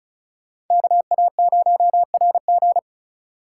Morse code, heard as a keyed tone, about 46 wpm, 700 Hz. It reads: KA0RG